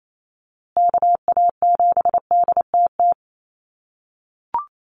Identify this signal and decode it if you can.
Morse code, heard as a keyed tone, about 28 wpm, 700 Hz. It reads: KA7DTT
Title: USA Call Signs Repeated 2x 28wpm